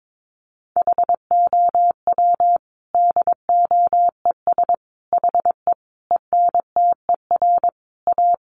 Morse code, heard as a keyed tone, about 22 wpm, 700 Hz. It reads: HOW DOES HE ENTER A
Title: Sentences from Top 1300 Words 22wpm